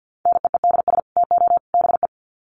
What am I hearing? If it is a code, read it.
Morse code, heard as a keyed tone, about 50 wpm, 700 Hz. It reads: NEEDS TO BE
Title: Sets of 3 Words 50wpm